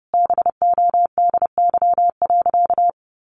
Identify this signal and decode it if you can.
Morse code, heard as a keyed tone, about 30 wpm, 700 Hz. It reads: BODY.
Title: A Sound Of Thunder 30wpm